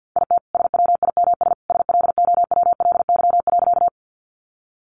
Morse code, heard as a keyed tone, about 48 wpm, 700 Hz. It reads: IT SWIMS SLOWLY.
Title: Easy Phrases 50wpm